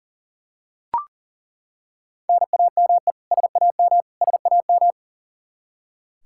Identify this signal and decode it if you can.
Morse code, heard as a keyed tone, about 40 wpm, 700 Hz. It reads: NAME SAM SAM
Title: Practice QSO - Exchange 1 - Beginning - 40wpm